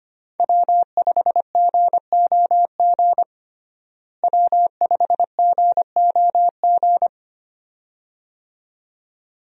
Morse code, heard as a keyed tone, about 25 wpm, 700 Hz. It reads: W5GOG W5GOG
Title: USA Call Signs - 5 Characters - Repeated 2x - 25wpm